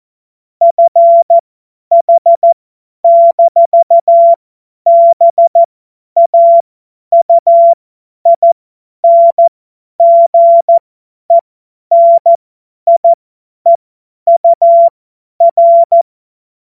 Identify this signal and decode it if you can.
Morse code, heard as a keyed tone, about 14 wpm, spaced as for 10 wpm, 680 Hz. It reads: FH-BAUINGENIEUR
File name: Morsekurs / Witze / Witz 31. Mai 2022 50 F70